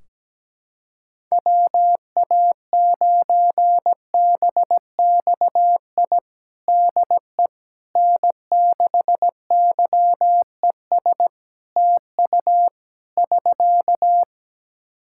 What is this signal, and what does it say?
Morse code, heard as a keyed tone, about 17 wpm, 700 Hz. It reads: WA9BXI DE N6YES TU <SK>
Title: Practice QSO - Exchange 4 - Ending - 17wpm